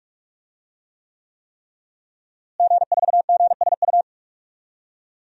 Morse code, heard as a keyed tone, about 47 wpm, 700 Hz. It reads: G4GSU
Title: DXCC-100 Call Signs - 5 Characters - Repeated 3x - 45wpm